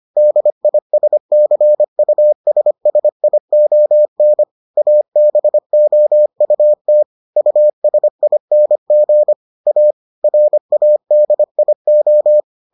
Morse code, heard as a keyed tone, about 25 wpm, 600 Hz. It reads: DISCUSSION ABOUT USING A RADIO